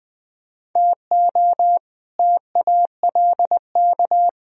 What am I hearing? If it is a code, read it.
Morse code, heard as a keyed tone, about 20 wpm, 700 Hz. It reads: TO TALK
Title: Speed-Racing - Sets of 2 Words 17wpm